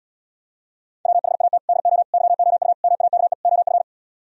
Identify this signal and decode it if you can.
Morse code, heard as a keyed tone, about 78 wpm, 700 Hz. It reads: WHAT DO YOU MAKE OF